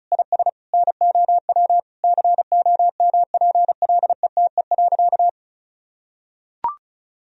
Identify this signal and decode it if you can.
Morse code, heard as a keyed tone, about 36 wpm, 700 Hz. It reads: IS NOW COMPLETE.
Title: Sentences from Top 400 Words and Abbreviations 35wpm